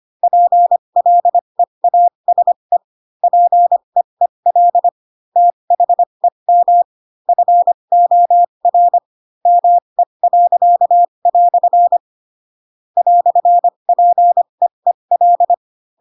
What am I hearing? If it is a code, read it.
Morse code, heard as a keyed tone, about 25 wpm, spaced as for 22 wpm, 700 Hz. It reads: PLEASE PEEL THEM FOR ME." "PEEL
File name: Pinocchio / Pinocchio ch7 18 25 ik1hge